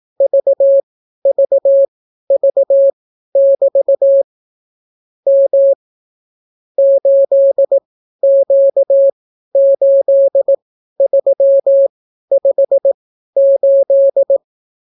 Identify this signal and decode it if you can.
Morse code, heard as a keyed tone, about 18 wpm, spaced as for 12 wpm, 550 Hz. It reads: VVV<BT> M 8Q8358